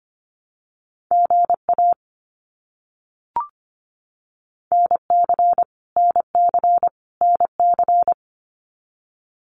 Morse code, heard as a keyed tone, about 25 wpm, 700 Hz. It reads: GA NC NC NC
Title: US States Repeated 3x 25wpm